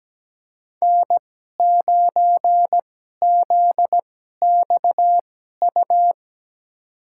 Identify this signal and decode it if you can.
Morse code, heard as a keyed tone, about 17 wpm, spaced as for 12 wpm, 700 Hz. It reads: N9ZXU
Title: USA Call Signs - 5 Characters - 2x Character Spacing - 17wpm